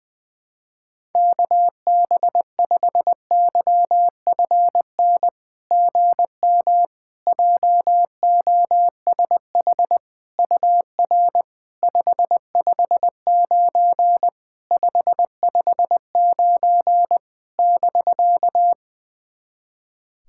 Morse code, heard as a keyed tone, about 20 wpm, 700 Hz. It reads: KB5YFN GM JOSH UR 559 559 <BK>
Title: SOTA QSO Practice 20wpm